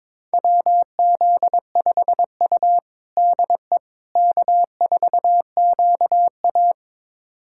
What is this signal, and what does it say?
Morse code, heard as a keyed tone, about 22 wpm, 700 Hz. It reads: WZ5U DE K4QA
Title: USA Callsign DE Callsign - 4 Characters - 22wpm